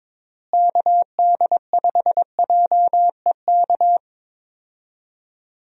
Morse code, heard as a keyed tone, about 22 wpm, 700 Hz. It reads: KD5JEK
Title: USA Call Signs - 6 Characters - Repeated 3x - 22wpm